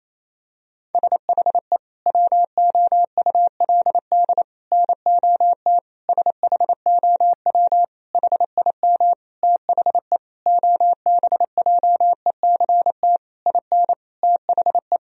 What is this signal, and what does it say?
Morse code, heard as a keyed tone, about 28 wpm, 700 Hz. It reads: SHE WOULD NOT SHOW HIM THE OBJECT IN THE
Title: Sentences from Top 1000 Words - Review Part 3 - 28wpm